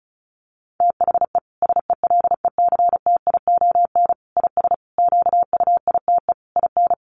Morse code, heard as a keyed tone, about 34 wpm, 700 Hz. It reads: THE SELECTION IS QUITE IN
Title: Sentences from Top 1400 Words 35wpm